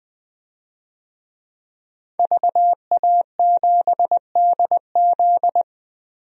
Morse code, heard as a keyed tone, about 20 wpm, 700 Hz. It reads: VA7DZ
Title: DXCC-100 Call Signs - 5 Characters - 20wpm